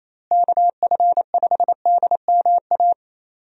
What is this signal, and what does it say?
Morse code, heard as a keyed tone, about 28 wpm, 700 Hz. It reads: KF5DMA
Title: USA Call Signs - 6 Characters - Repeated 3x - 28wpm